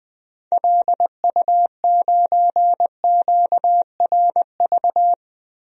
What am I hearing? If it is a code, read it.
Morse code, heard as a keyed tone, about 21 wpm, 700 Hz. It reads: LU9QRV
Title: DXCC-100 Call Signs - 6 Characters - Repeated 2x - 20wpm